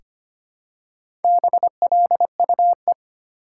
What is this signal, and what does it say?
Morse code, heard as a keyed tone, about 25 wpm, 700 Hz. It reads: BLUE
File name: practice / 25wpm / Easy Words with Hints - 25wpm